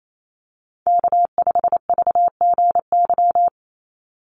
Morse code, heard as a keyed tone, about 28 wpm, 700 Hz. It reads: K5VGY
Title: USA Call Signs 28wpm